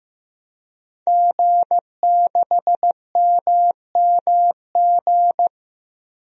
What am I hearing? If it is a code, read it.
Morse code, heard as a keyed tone, about 15 wpm, 700 Hz. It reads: G6MMG
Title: DXCC-100 Call Signs - 5 Characters - Repeated 3x - 15wpm